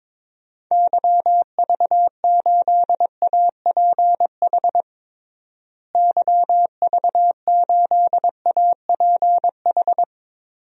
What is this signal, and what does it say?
Morse code, heard as a keyed tone, about 22 wpm, 700 Hz. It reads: YV8APH YV8APH
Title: DXCC-100 Call Signs - 6 Characters - Repeated 3x - 22wpm